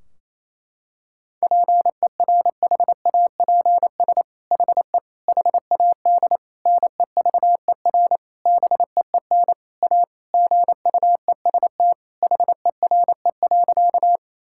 Morse code, heard as a keyed tone, about 27 wpm, 700 Hz. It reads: PERHAPS HE HAD NEVER BEEN A GUEST HERE.
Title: Sentences from Top 1600 Words 28wpm